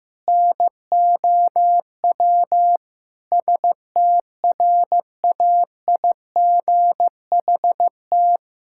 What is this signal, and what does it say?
Morse code, heard as a keyed tone, about 15 wpm, 700 Hz. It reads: NOW STRAIGHT
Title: Sentences from Top 1000 Words - Review Part 6 - 15wpm